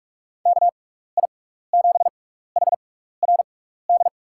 Morse code, heard as a keyed tone, about 44 wpm, spaced as for 15 wpm, 700 Hz. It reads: KI7HRD
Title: USA Call Signs - 6x Character Spacing - 45wpm